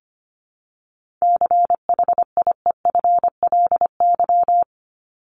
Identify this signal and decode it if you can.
Morse code, heard as a keyed tone, about 25 wpm, 700 Hz. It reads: CHIEFLY
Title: LY Suffix - Top 100 Words with Base Words 25wpm